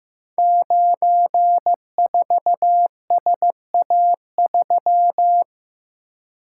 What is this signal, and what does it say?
Morse code, heard as a keyed tone, about 15 wpm, 700 Hz. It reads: 94SA3
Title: Code Groups - 5 Letter:Numbers - 15wpm